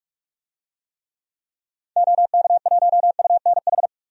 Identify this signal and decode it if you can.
Morse code, heard as a keyed tone, about 46 wpm, 700 Hz. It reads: OK1UNH
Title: DXCC-100 Call Signs - 6 Characters - 45wpm